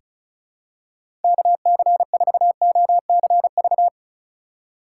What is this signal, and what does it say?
Morse code, heard as a keyed tone, about 36 wpm, 700 Hz. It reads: KC4OCV
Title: USA Call Signs - 6 Characters - 8x Character Spacing - 35wpm